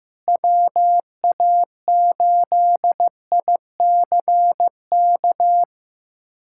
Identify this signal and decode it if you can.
Morse code, heard as a keyed tone, about 15 wpm, 700 Hz. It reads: WA8ICK
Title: USA Callsign DE Callsign - 6 Characters - 15wpm